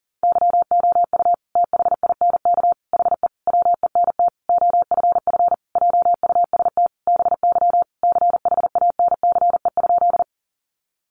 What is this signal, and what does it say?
Morse code, heard as a keyed tone, about 40 wpm, 700 Hz. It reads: YOU THINK HE WENT OFF JUST BY CHANCE?
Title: Sentences from Top 800 Words 40wpm